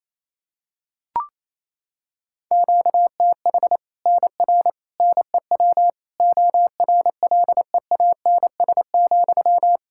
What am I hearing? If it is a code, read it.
Morse code, heard as a keyed tone, about 28 wpm, 700 Hz. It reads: QTH NR NEW ORLEANS,
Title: Practice QSO - Exchange 1 - Beginning - 28wpm